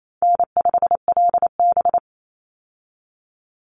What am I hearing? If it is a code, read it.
Morse code, heard as a keyed tone, about 28 wpm, 700 Hz. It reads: N5LB
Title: Call Signs - 28wpm